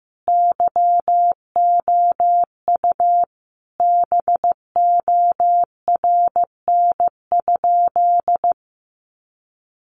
Morse code, heard as a keyed tone, about 14 wpm, 700 Hz. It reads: YOU BORN?
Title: Sentences from Top 1000 Words 15wpm